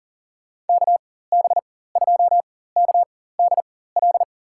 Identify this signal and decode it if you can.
Morse code, heard as a keyed tone, about 40 wpm, spaced as for 19 wpm, 700 Hz. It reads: KB2KDL
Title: USA Call Signs - 6 Characters - 4x Character Spacing - 40wpm